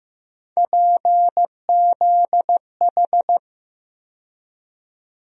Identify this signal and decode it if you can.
Morse code, heard as a keyed tone, about 15 wpm, 700 Hz. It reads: PZH
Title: USA Call Sign Postfixes - 3 Letters 15wpm